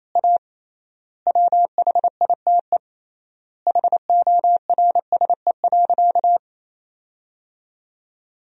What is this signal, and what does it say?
Morse code, heard as a keyed tone, about 28 wpm, 700 Hz. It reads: A WHITE HORSE.
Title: 3x Word-Spacing - Sentences from Top 300 Words - 28wpm